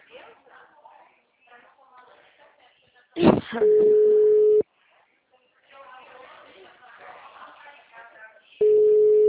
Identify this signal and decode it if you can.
Ringback tone (Europe, 425 Hz), with speech.